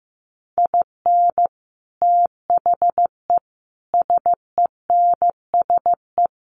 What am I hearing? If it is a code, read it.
Morse code, heard as a keyed tone, about 15 wpm, 700 Hz. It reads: IN THE SENSE
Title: Sets of 4 Words 15wpm